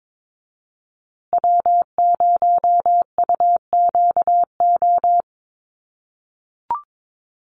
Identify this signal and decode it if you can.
Morse code, heard as a keyed tone, about 22 wpm, 700 Hz. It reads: W0UQO
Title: USA Call Signs Repeated 2x 22wpm